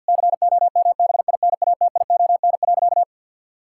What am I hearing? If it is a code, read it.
Morse code, heard as a keyed tone, about 51 wpm, 700 Hz. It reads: COMBINATION.
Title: Sentences from Top 1000 Words - Review Part 6 - 50wpm